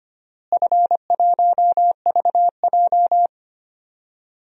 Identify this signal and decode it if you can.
Morse code, heard as a keyed tone, about 25 wpm, 700 Hz. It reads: F1VJ
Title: DXCC-100 Call Signs - 4 Characters - Repeated 3x - 25wpm